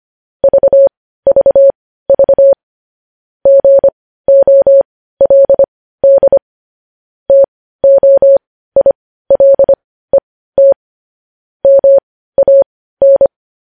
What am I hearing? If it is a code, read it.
Morse code, heard as a keyed tone, about 25 wpm, spaced as for 15 wpm, 550 Hz. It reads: VVV GOLD TOILET MAN
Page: news headlines in morse code at 15 wpm